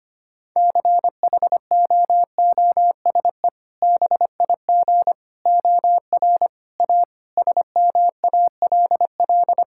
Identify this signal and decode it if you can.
Morse code, heard as a keyed tone, about 25 wpm, 700 Hz. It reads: CHOOSE BIG OR A SMALL
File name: practice / 25wpm / Sentences with 2-Word Phrases - 25wpm